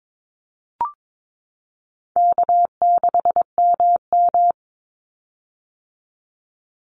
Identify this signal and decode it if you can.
Morse code, heard as a keyed tone, about 22 wpm, 700 Hz. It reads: K6MM
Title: Easy Call Signs - 22wpm